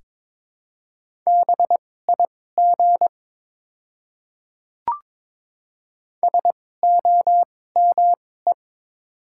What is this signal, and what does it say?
Morse code, heard as a keyed tone, about 22 wpm, spaced as for 16 wpm, 700 Hz. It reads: BIG SOME
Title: 2x Character Spacing - Top 200 Words - 22wpm